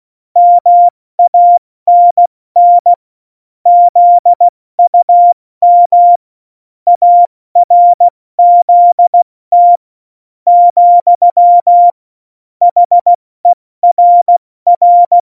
Morse code, heard as a keyed tone, about 16 wpm, 700 Hz. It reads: MANN ZUM ARZT , HERR
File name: Morsekurs / Witze / Witz 06 March 2018, 70 F80